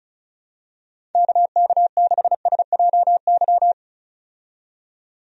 Morse code, heard as a keyed tone, about 34 wpm, 700 Hz. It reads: KK6SJY
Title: USA Call Signs - 6 Characters - 35wpm